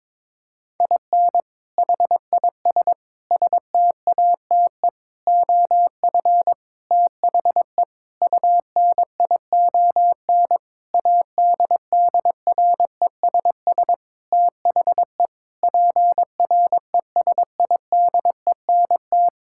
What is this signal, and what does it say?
Morse code, heard as a keyed tone, about 22 wpm, 700 Hz. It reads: IN HIS STATE OF THE UNION ADDRESS THE PRESIDENT
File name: Sentences with 7-Word Phrases - 22wpm